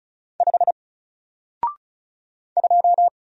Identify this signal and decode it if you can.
Morse code, heard as a keyed tone, about 34 wpm, 700 Hz. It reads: 5 2
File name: Single-Letter-Number-35wpm